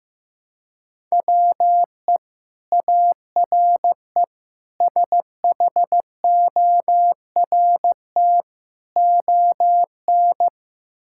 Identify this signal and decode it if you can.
Morse code, heard as a keyed tone, about 15 wpm, 700 Hz. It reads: WE ARE SHORT ON
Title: Sentences from Top 1000 Words - Review Part 5 - 15wpm